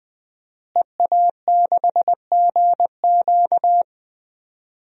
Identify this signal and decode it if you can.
Morse code, heard as a keyed tone, about 20 wpm, 700 Hz. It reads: EA6GQ
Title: DXCC-100 Call Signs - 5 Characters - Repeated 2x - 20wpm